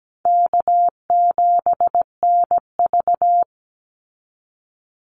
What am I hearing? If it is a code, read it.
Morse code, heard as a keyed tone, about 17 wpm, 700 Hz. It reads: K7NV
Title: Easy Call Signs Repeated 2x - 17wpm